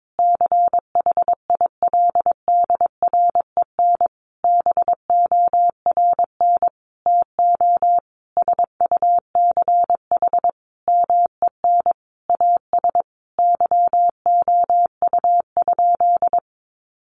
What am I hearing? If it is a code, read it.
Morse code, heard as a keyed tone, about 22 wpm, 700 Hz. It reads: CHILDREN BORN TO SUCH MEN AS YOU?
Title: Sentences from Top 1000 Words 22wpm